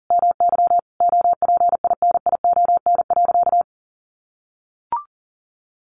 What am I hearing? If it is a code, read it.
Morse code, heard as a keyed tone, about 40 wpm, 700 Hz. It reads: MY OPINION.
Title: Sentences from Top 1400 Words 40wpm